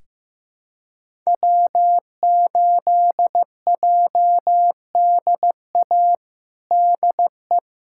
Morse code, heard as a keyed tone, about 15 wpm, 700 Hz. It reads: W8JDA DE
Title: USA Callsign DE Callsign - 5 Characters - 15wpm